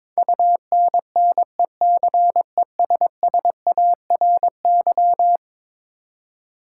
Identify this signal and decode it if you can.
Morse code, heard as a keyed tone, about 21 wpm, 700 Hz. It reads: UNNECESSARY
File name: practice / 22wpm / Eleven Letter Words - 22wpm